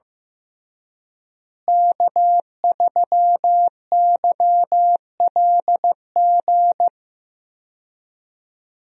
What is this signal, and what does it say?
Morse code, heard as a keyed tone, about 16 wpm, 700 Hz. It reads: K3YLG
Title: USA Call Signs - 5 Characters - Repeated 2x - 15wpm